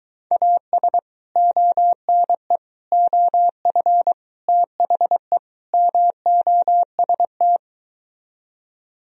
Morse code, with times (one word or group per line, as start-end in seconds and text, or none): 0.3-1.0 AS
1.4-2.6 ONE
2.9-4.1 OF
4.5-5.4 THE
5.7-7.6 MOST